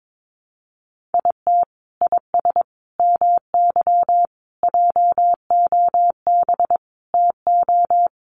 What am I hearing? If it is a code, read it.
Morse code, heard as a keyed tone, about 22 wpm, 700 Hz. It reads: IT IS MY JOB TO
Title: Sentences from Top 600 Words 22wpm